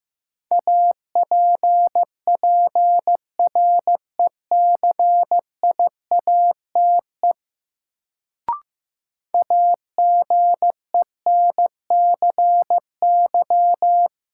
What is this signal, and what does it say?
Morse code, heard as a keyed tone, about 15 wpm, 700 Hz. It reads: APPRECIATE AGENCY
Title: Sentences from Top 1300 Words - New Word Drill - 15wpm